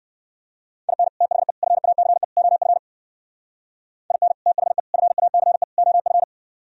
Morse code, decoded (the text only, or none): IN THE FACE OF IN THE FACE OF